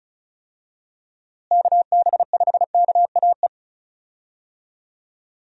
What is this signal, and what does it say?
Morse code, heard as a keyed tone, about 36 wpm, 700 Hz. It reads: KB5KAE
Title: USA Call Signs - 6 Characters - Repeated 2x - 35wpm